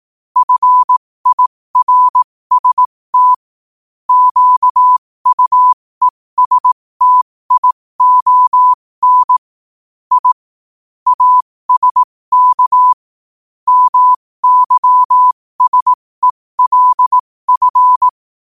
Morse code, heard as a keyed tone, about 18 wpm, spaced as for 15 wpm, 1000 Hz. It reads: FIRST QUESTION I ASK MYSELF
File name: qotdfiles / 15wpm